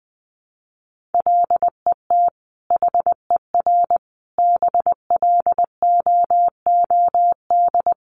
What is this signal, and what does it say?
Morse code, heard as a keyed tone, about 20 wpm, 700 Hz. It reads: LET HER BLOOD
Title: Sentences from Top 1200 Words 20wpm